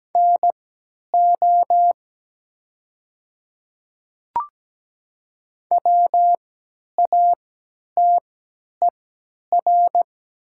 Morse code, heard as a keyed tone, about 17 wpm, spaced as for 10 wpm, 700 Hz. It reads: NO WATER
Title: 3x Character Spacing - Top 100 Words - 17wpm